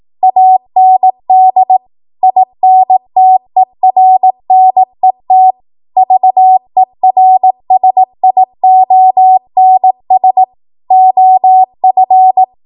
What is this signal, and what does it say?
Morse code, heard as a keyed tone, about 18 wpm, 750 Hz. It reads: AND INTERNET VERSIONS OF
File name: Archive / 18 WPM / 18WPM